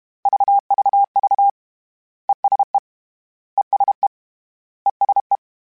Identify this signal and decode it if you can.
Morse code, heard as a keyed tone, about 32 wpm, 800 Hz. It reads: VVV ESE ESE ESE
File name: General / General-mp3s / ESE ING @30